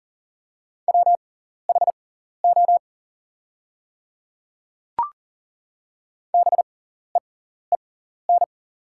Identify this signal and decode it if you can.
Morse code, heard as a keyed tone, about 40 wpm, spaced as for 14 wpm, 700 Hz. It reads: WHO BEEN
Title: 6x Character Spacing - Top 100 Words - 40wpm